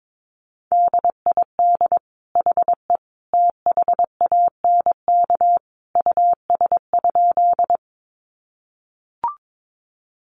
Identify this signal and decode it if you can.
Morse code, heard as a keyed tone, about 22 wpm, 700 Hz. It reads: DID HE THANK US?
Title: Sentences from Top 600 Words 22wpm